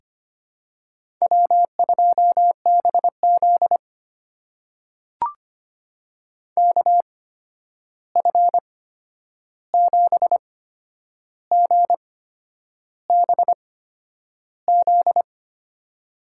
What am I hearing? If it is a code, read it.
Morse code, heard as a keyed tone, about 25 wpm, spaced as for 9 wpm, 700 Hz. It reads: W2BZ KF7GBZ
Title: USA Call Signs - 8x Character Spacing - 25wpm